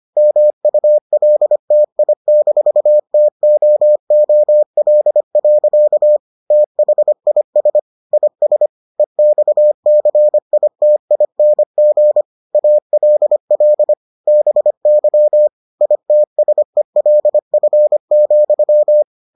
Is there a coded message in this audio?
Yes: Morse code from 0.2 to 19.0 s, heard as a keyed tone, about 25 wpm, 600 Hz. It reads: MULTI-TOOL. THIS IS EXCITING ALL BY ITSELF,